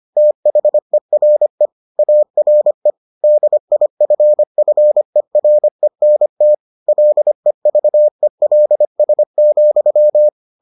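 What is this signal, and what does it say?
Morse code, heard as a keyed tone, about 25 wpm, 600 Hz. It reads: THERE ARE DIFFERENT LEVELS,
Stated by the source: Morse edition of the Foundations of Amateur Radio episodes.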